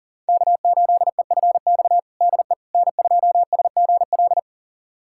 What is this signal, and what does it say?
Morse code, heard as a keyed tone, about 40 wpm, 700 Hz. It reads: K8EFX DE N2SGL
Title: USA Callsign DE Callsign - 5 Characters - 40wpm